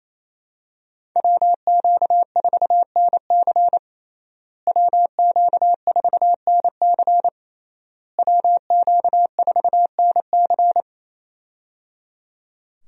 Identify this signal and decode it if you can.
Morse code, heard as a keyed tone, about 28 wpm, 700 Hz. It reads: WQ4NC WQ4NC WQ4NC
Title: USA Call Signs - 5 Characters - Repeated 3x - 28wpm